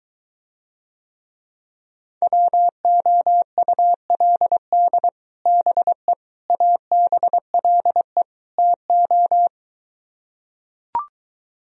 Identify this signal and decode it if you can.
Morse code, heard as a keyed tone, about 23 wpm, 700 Hz. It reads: WOULD BE ABLE TO